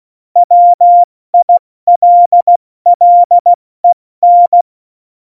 Morse code, heard as a keyed tone, about 16 wpm, spaced as for 14 wpm, 700 Hz. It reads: WILLEN
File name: Morsekurs / Witze / Witz 18. Februar 2020, 70 F80